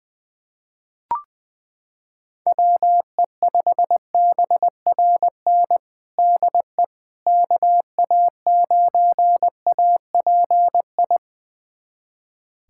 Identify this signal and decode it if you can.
Morse code, heard as a keyed tone, about 21 wpm, 700 Hz. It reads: WE5BRN DE KA9API
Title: USA Callsign DE Callsign - 6 Characters - 20wpm